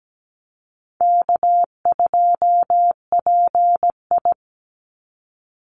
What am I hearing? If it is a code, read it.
Morse code, heard as a keyed tone, about 17 wpm, 700 Hz. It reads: K2PI
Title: Easy Call Signs Repeated 3x - 17wpm